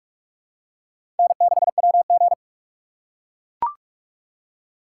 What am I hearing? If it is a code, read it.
Morse code, heard as a keyed tone, about 44 wpm, 700 Hz. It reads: N6WG
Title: K1USN SST QSO Practice - 45wpm